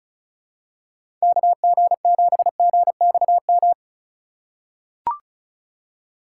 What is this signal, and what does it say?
Morse code, heard as a keyed tone, about 36 wpm, 700 Hz. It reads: KG7GXM
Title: USA Call Signs - 6 Characters - 2x Character Spacing - 35wpm